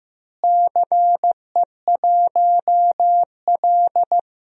Morse code, heard as a keyed tone, about 15 wpm, 700 Hz. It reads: CE1L
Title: DXCC-100 Call Signs - 4 Characters - Repeated 3x - 15wpm